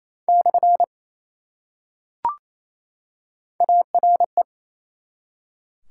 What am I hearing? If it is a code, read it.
Morse code, heard as a keyed tone, about 28 wpm, 700 Hz. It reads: / ARE
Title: Top 180 QSO Elements 28wpm